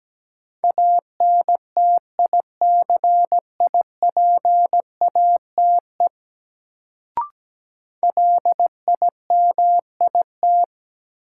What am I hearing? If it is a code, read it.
Morse code, heard as a keyed tone, about 17 wpm, 700 Hz. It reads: ANTICIPATE LIMIT
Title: Sentences from Top 1500 Words - New Word Drill - 17wpm